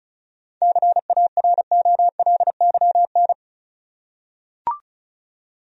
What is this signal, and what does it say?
Morse code, heard as a keyed tone, about 36 wpm, 700 Hz. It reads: CAROLYN
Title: USA Common Names - 35wpm